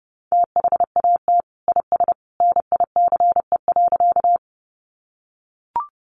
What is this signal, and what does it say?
Morse code, heard as a keyed tone, about 30 wpm, 700 Hz. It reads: THAT IS NICE.
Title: Sentences from Top 1100 Words 30wpm